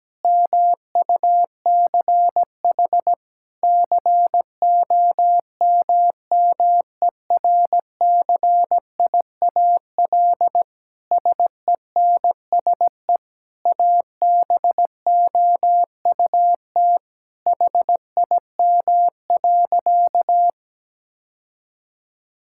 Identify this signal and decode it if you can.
Morse code, heard as a keyed tone, about 17 wpm, 700 Hz. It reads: MUCH COMMERCIAL SENSE ABOUT HIM.
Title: Sentences from Top 1000 Words - Review Part 5 - 17wpm